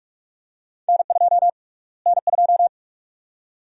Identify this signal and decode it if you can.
Morse code, heard as a keyed tone, about 46 wpm, 700 Hz. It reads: N2 N2
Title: USA Call Sign Prefixes - Repeated 2x - 45wpm